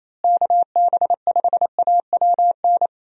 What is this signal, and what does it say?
Morse code, heard as a keyed tone, about 28 wpm, 700 Hz. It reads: KB5AWN
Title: USA Call Signs - 6 Characters - Repeated 3x - 28wpm